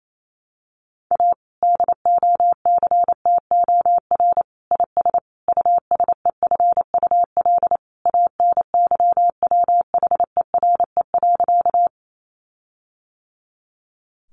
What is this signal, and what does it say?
Morse code, heard as a keyed tone, about 28 wpm, 700 Hz. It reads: A DOCTOR IS USEFUL ANYWHERE.
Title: Sentences from Top 1500 Words 28wpm